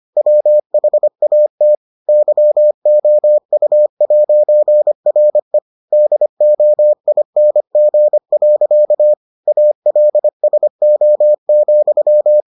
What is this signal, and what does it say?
Morse code, heard as a keyed tone, about 25 wpm, 600 Hz. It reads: WHAT YOU'RE DOING. ALSO,